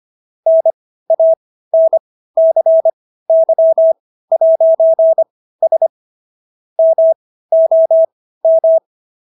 Morse code, heard as a keyed tone, about 25 wpm, spaced as for 15 wpm, 650 Hz. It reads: NANCY'S MOM